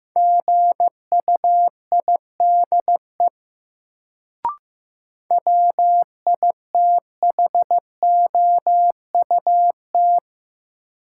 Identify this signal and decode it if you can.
Morse code, heard as a keyed tone, about 15 wpm, 700 Hz. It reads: GUIDE WITHOUT
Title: Sentences from Top 600 Words - New Word Drill - 15wpm